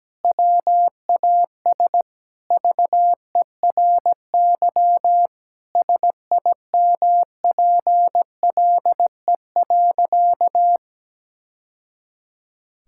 Morse code, heard as a keyed tone, about 17 wpm, 700 Hz. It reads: WAS VERY SIMPLE.
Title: Sentences from Top 1000 Words - Review Part 6 - 17wpm